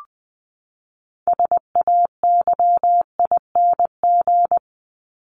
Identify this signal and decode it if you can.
Morse code, heard as a keyed tone, about 20 wpm, 700 Hz. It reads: SAYING
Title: ING Suffix - Top 100 Words with Base Words 20wpm